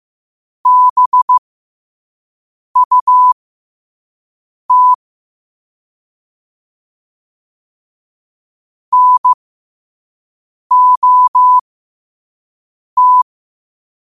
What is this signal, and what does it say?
Morse code, heard as a keyed tone, about 15 wpm, spaced as for 5 wpm, 1000 Hz. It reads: BUT NOT
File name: qotdfiles / 5wpm